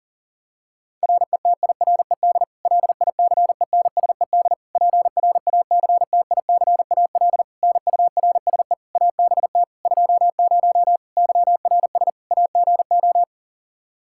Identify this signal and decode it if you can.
Morse code, heard as a keyed tone, about 40 wpm, 700 Hz. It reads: RETIRED LICENSED PRACTICAL NURSE ABT 20 YRS AGO
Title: Practice QSO - Exchange 3 - More Info - 40wpm